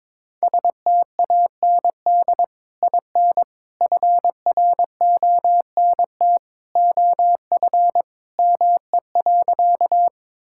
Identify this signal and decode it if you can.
Morse code, heard as a keyed tone, about 23 wpm, 700 Hz. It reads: STAND IN FRONT OF ME.
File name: practice / 22wpm / Sentences with 2-Word Phrases - 22wpm